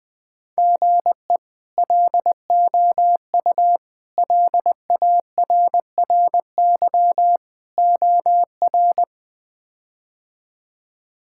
Morse code, heard as a keyed tone, about 20 wpm, 700 Hz. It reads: GE LOU LARRY OR
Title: K1USN SST QSO Practice - 20wpm